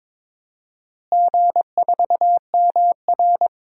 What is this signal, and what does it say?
Morse code, heard as a keyed tone, about 22 wpm, 700 Hz. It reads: G4MR
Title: DXCC-100 Call Signs - 4 Characters - Repeated 3x - 22wpm